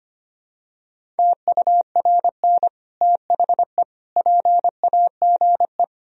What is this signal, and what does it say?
Morse code, heard as a keyed tone, about 25 wpm, 700 Hz. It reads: TURN THE PAGE
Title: Sentences from Top 1000 Words - Review Part 1 - 25wpm